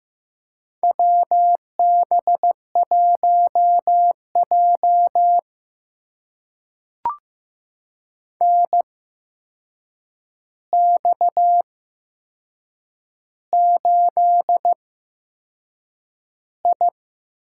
Morse code, heard as a keyed tone, about 15 wpm, spaced as for 6 wpm, 700 Hz. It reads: WB1J NX8I
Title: USA Call Signs - 4 Characters - 8x Character Spacing - 15wpm